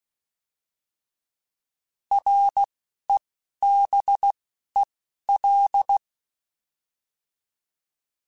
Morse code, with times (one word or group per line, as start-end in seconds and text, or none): 2.1-2.6 R
3.1-3.2 E
3.6-4.3 B
4.8-4.8 E
5.3-6.0 L